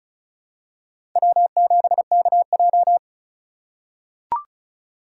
Morse code, heard as a keyed tone, about 34 wpm, 690 Hz. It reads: W7KJ
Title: USA Call Signs - 3x Character Spacing - 35wpm